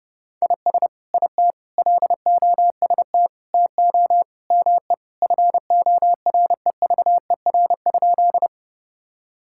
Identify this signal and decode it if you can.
Morse code, heard as a keyed tone, about 30 wpm, 700 Hz. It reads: IS IT LOST TO ME FOREVER?
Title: Sentences from Top 1600 Words 30wpm